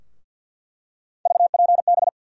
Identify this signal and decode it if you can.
Morse code, heard as a keyed tone, about 47 wpm, 700 Hz. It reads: VPB
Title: USA Call Sign Postfixes - 3 Letters 50wpm